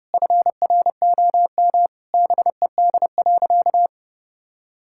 Morse code, heard as a keyed tone, about 30 wpm, 700 Hz. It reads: FROM BED.
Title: Sentences from Top 1000 Words - Review Part 6 - 30wpm